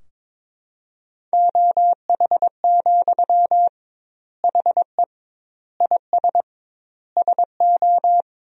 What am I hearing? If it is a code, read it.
Morse code, heard as a keyed tone, about 22 wpm, 700 Hz. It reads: OH, HE IS SO
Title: 2x Word-Spacing - Sentences from Top 500 Words - 22wpm